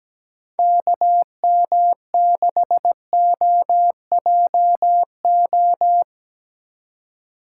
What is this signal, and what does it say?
Morse code, heard as a keyed tone, about 19 wpm, 700 Hz. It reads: KM6OJO
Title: Speed Racing - USA 2x3 Call Signs 17wpm